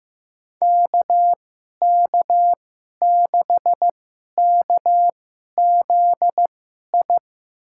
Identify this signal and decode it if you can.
Morse code, heard as a keyed tone, about 15 wpm, spaced as for 11 wpm, 700 Hz. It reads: KK6KZI
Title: USA Call Signs - 6 Characters - 2x Character Spacing - 15wpm